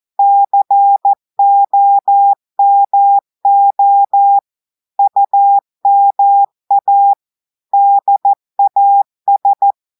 Morse code, heard as a keyed tone, about 14 wpm, 800 Hz. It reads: COMO UMA DAS
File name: audios / portugais / portugais texte en clair 1 vitesse 14 mots minutes